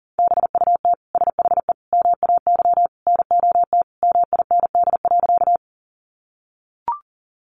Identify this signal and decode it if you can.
Morse code, heard as a keyed tone, about 40 wpm, 700 Hz. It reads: BUT SHE MAY NOT MIND.
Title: Sentences from Top 600 Words 40wpm